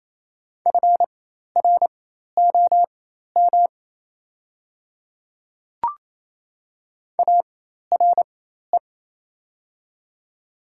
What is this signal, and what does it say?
Morse code, heard as a keyed tone, about 28 wpm, spaced as for 13 wpm, 700 Hz. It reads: FROM ARE
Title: 4x Character Spacing - Top 100 Words - 28wpm